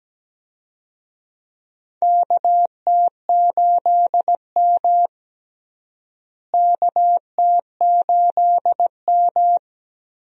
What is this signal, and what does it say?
Morse code, heard as a keyed tone, about 17 wpm, 700 Hz. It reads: KT8M KT8M
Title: USA Call Signs - 4 Characters - Repeated 3x - 17wpm